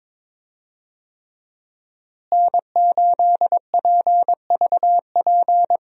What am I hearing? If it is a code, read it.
Morse code, heard as a keyed tone, about 21 wpm, 700 Hz. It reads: N8PVP
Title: USA Call Signs - 5 Characters - Repeated 3x - 22wpm